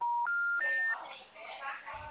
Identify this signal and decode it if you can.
Special information tone, with speech.